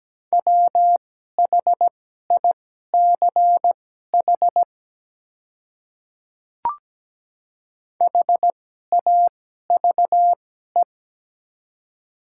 Morse code, heard as a keyed tone, about 17 wpm, spaced as for 12 wpm, 700 Hz. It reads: WHICH HAVE
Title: 2x Character Spacing - Top 100 Words - 17wpm